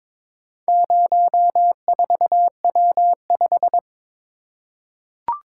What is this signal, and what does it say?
Morse code, heard as a keyed tone, about 22 wpm, 700 Hz. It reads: 04W5
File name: practice / 22wpm / Four Letter-Number Combinations - 22wpm